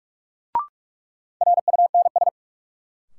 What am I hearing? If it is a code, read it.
Morse code, heard as a keyed tone, about 46 wpm, 700 Hz. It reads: RUNS
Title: Sentences from Top 600 Words - New Word Drill - 45wpm